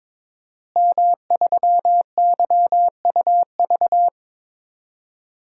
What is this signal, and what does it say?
Morse code, heard as a keyed tone, about 22 wpm, 700 Hz. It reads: M3YUV
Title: DXCC-100 Call Signs - 5 Characters - Repeated 2x - 22wpm